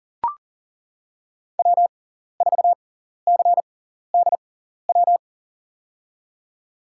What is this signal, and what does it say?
Morse code, heard as a keyed tone, about 40 wpm, spaced as for 14 wpm, 700 Hz. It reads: W4CDW
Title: USA Call Signs - 5 Characters - 6x Character Spacing - 40wpm